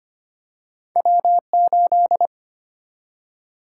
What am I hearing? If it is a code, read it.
Morse code, heard as a keyed tone, about 25 wpm, 700 Hz. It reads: W8
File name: Two Letter-Number Combinations - 25wpm